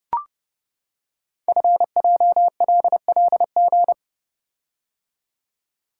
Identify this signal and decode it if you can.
Morse code, heard as a keyed tone, about 30 wpm, 700 Hz. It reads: FJLLG
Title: Code Groups - 5 Letters - 30wpm